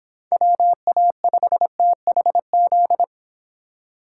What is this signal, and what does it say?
Morse code, heard as a keyed tone, about 26 wpm, 700 Hz. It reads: WA5THZ